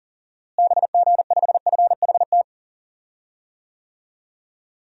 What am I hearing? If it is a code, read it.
Morse code, heard as a keyed tone, about 40 wpm, 700 Hz. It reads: BG5FHT
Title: DXCC-100 Call Signs - 6 Characters - Repeated 3x - 40wpm